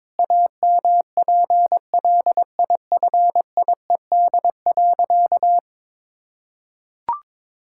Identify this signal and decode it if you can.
Morse code, heard as a keyed tone, about 21 wpm, 700 Hz. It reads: AMPLIFIED.
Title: Pangram Challenge v2 - 22wpm